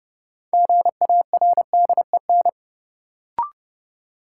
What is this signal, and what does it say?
Morse code, heard as a keyed tone, about 31 wpm, 700 Hz. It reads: GARDEN
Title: Sentences from Top 700 Words - New Word Drill - 30wpm